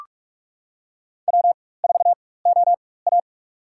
Morse code, heard as a keyed tone, about 44 wpm, spaced as for 21 wpm, 690 Hz. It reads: W4OA
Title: USA Call Signs - 4x Character Spacing - 45wpm